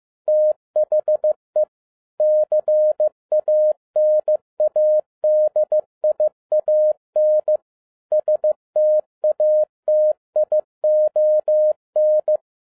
Morse code, heard as a keyed tone, about 15 wpm, 610 Hz. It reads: THE CANADIAN STATION
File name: 2011-10-mp3 / n07-A DOT 15wpm.mp3